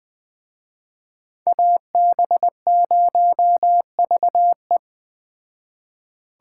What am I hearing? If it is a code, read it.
Morse code, heard as a keyed tone, about 20 wpm, 700 Hz. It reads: AB0VE